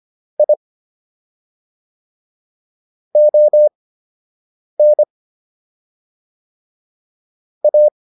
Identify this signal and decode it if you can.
Morse code, heard as a keyed tone, about 25 wpm, spaced as for 7 wpm, 600 Hz. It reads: I ON A